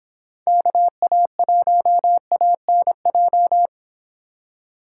Morse code, heard as a keyed tone, about 29 wpm, 700 Hz. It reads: KA1ANJ